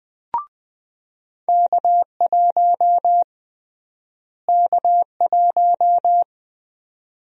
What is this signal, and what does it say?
Morse code, heard as a keyed tone, about 20 wpm, 700 Hz. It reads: K1 K1
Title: USA Call Sign Prefixes - Repeated 2x - 20wpm